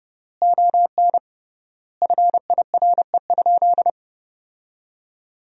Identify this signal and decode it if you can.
Morse code, heard as a keyed tone, about 30 wpm, 700 Hz. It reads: ON FIRE?
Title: 3x Word-Spacing - Sentences from Top 400 Words - 30wpm